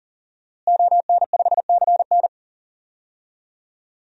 Morse code, heard as a keyed tone, about 37 wpm, 700 Hz. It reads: ON5CN
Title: DXCC-100 Call Signs - 5 Characters - Repeated 3x - 40wpm